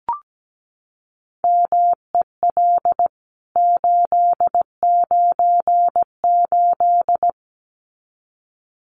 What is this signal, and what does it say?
Morse code, heard as a keyed tone, about 17 wpm, 700 Hz. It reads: MEL 898
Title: CWT Exchange 17wpm